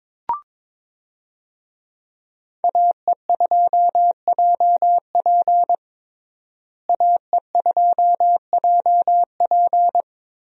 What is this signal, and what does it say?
Morse code, heard as a keyed tone, about 22 wpm, 700 Hz. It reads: AE2JP AE2JP
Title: USA Call Signs - 5 Characters - Repeated 3x - 22wpm